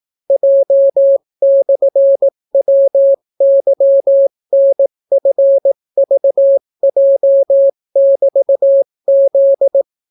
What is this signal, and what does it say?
Morse code, heard as a keyed tone, about 18 wpm, 550 Hz. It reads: J/WYNFVJ<BT>Z